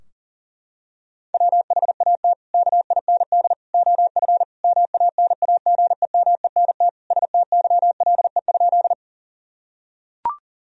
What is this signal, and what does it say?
Morse code, heard as a keyed tone, about 40 wpm, 700 Hz. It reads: WHAT KIND OF MANAGEMENT STYLE?
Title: Sentences from Top 1000 Words - Review Part 4 - 40wpm